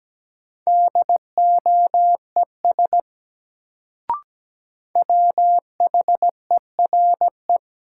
Morse code, heard as a keyed tone, about 17 wpm, 700 Hz. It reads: DOES WHERE
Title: Sets of 2 Words - New Word Drill - 17wpm